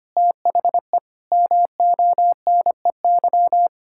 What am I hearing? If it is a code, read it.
Morse code, heard as a keyed tone, about 25 wpm, 700 Hz. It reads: THE MONEY
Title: Pangram Challenge v2 - 25wpm